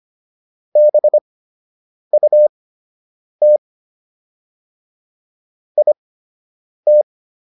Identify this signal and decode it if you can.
Morse code, heard as a keyed tone, about 25 wpm, spaced as for 8 wpm, 600 Hz. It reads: BUT IT